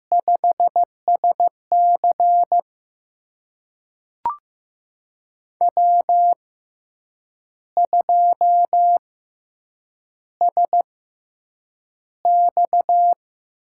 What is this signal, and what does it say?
Morse code, heard as a keyed tone, about 15 wpm, spaced as for 7 wpm, 700 Hz. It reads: 5SC W2SX
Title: USA Call Signs - 4 Characters - 6x Character Spacing - 15wpm